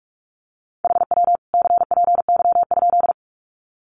Morse code, heard as a keyed tone, about 44 wpm, 700 Hz. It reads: HW CPY?
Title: Common QSO Phrases 45wpm